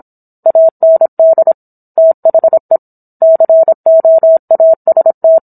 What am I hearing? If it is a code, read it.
Morse code, heard as a keyed tone, about 27 wpm, 650 Hz. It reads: AND THE COAST